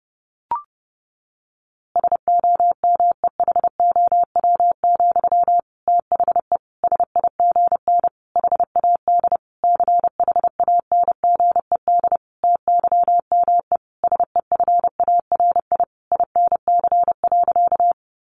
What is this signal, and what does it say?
Morse code, heard as a keyed tone, about 30 wpm, 700 Hz. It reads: SOMEHOW, THE SIGN HAD CHANGED TYME SEFARI INC.
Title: A Sound Of Thunder 30wpm